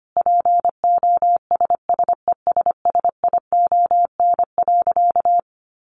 Morse code, heard as a keyed tone, about 25 wpm, 700 Hz. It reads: POSSESSION.
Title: Sentences from Top 1400 Words 25wpm